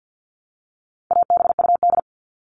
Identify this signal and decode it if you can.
Morse code, heard as a keyed tone, about 50 wpm, 700 Hz. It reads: ABUD
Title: All Cut Numbers - 4-Digit 50wpm